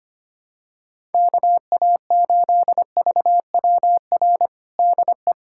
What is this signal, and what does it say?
Morse code, heard as a keyed tone, about 25 wpm, 700 Hz. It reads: KA8VWR DE
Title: USA Callsign DE Callsign - 6 Characters - 25wpm